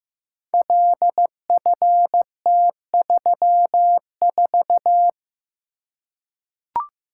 Morse code, heard as a keyed tone, about 15 wpm, 700 Hz. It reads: LFT34
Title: Code Groups - 5 Letter:Numbers - 15wpm